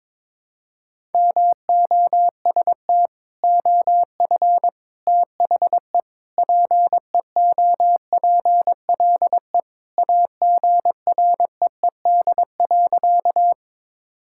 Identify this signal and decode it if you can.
Morse code, heard as a keyed tone, about 21 wpm, 700 Hz. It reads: MOST OF THE PEOPLE AGREED.
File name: practice / 22wpm / Sentences with 3-Word Phrases - 22wpm